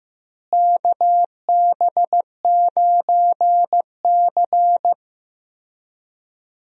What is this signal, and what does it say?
Morse code, heard as a keyed tone, about 15 wpm, 700 Hz. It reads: KB9C
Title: USA Call Signs - 4 Characters - Repeated 3x - 15wpm